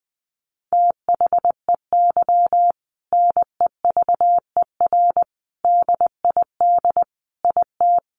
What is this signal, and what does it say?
Morse code, heard as a keyed tone, about 20 wpm, 700 Hz. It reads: THEY NEVER DID IT
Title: Sentences from Top 300 Words 20wpm